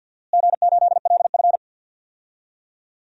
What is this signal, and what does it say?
Morse code, heard as a keyed tone, about 52 wpm, 700 Hz. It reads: G8LF
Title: DXCC-100 Call Signs - 4 Characters - Repeated 3x - 50wpm